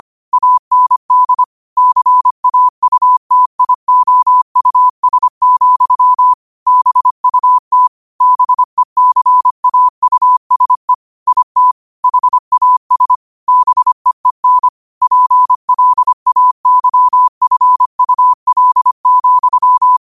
Morse code, heard as a keyed tone, about 25 wpm, 1000 Hz. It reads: AND CAUTIOUS, BUT BECAUSE IT HAS BEEN PLAYFUL,
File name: qotdfiles / 25wpm